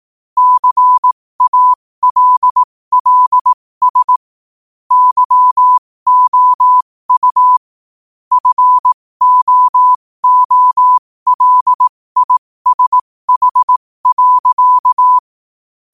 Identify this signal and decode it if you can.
Morse code, heard as a keyed tone, about 18 wpm, spaced as for 15 wpm, 1000 Hz. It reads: CALLS YOU FOOLISH.